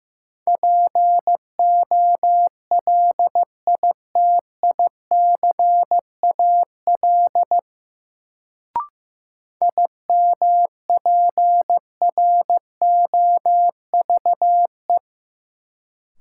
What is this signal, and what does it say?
Morse code, heard as a keyed tone, about 15 wpm, 700 Hz. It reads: POLITICAL IMPROVE
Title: Sentences from Top 800 Words - New Word Drill - 15wpm